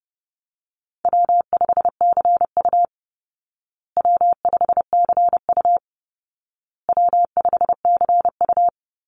Morse code, heard as a keyed tone, about 30 wpm, 700 Hz. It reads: W5CU W5CU W5CU
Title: Call Signs Repeated 2x - 30wpm